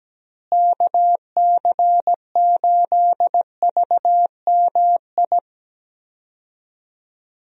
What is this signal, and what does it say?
Morse code, heard as a keyed tone, about 17 wpm, 700 Hz. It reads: KC8VMI
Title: USA Call Signs - 6 Characters - Repeated 3x - 17wpm